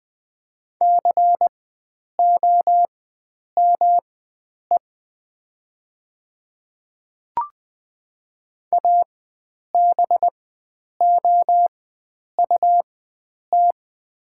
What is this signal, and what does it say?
Morse code, heard as a keyed tone, about 20 wpm, spaced as for 9 wpm, 700 Hz. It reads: COME ABOUT
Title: 4x Character Spacing - Top 100 Words - 20wpm